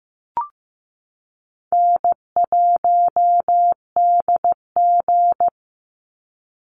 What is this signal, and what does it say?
Morse code, heard as a keyed tone, about 15 wpm, 700 Hz. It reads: N1DG
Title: Easy Call Signs Repeated 3x - 15wpm